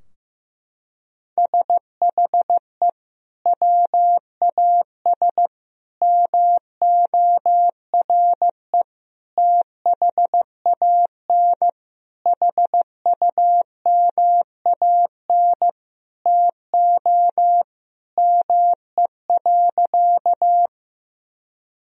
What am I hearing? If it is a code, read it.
Morse code, heard as a keyed tone, about 15 wpm, 700 Hz. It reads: SHE WAS MORE THAN HUMAN TO ME.
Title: Sentences from Top 1000 Words - Review Part 3 - 15wpm